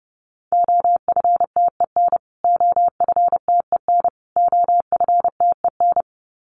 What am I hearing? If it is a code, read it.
Morse code, heard as a keyed tone, about 30 wpm, 700 Hz. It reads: OFTEN OFTEN OFTEN
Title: Top 500 Words Repeated 30wpm